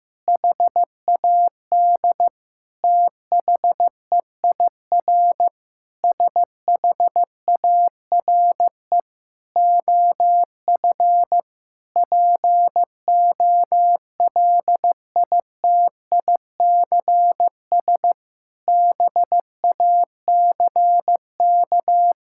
Morse code, heard as a keyed tone, about 15 wpm, 700 Hz. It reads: HAD THEIR SHARE OF POLITICS BACK
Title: Sentences from Top 1000 Words - Review Part 6 - 15wpm